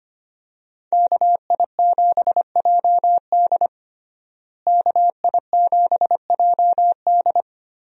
Morse code, heard as a keyed tone, about 25 wpm, 700 Hz. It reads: KI7JD KI7JD
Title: USA Call Signs - 5 Characters - Repeated 2x - 25wpm